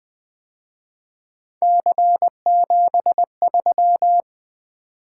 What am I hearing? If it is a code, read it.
Morse code, heard as a keyed tone, about 24 wpm, 700 Hz. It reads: C73
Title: Speed-Racing - Three Letter-Number Combinations - 20wpm